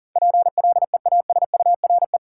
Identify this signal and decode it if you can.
Morse code, heard as a keyed tone, about 37 wpm, 700 Hz. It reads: PLEASURE
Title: Sentences from Top 1600 Words - New Word Drill - 40wpm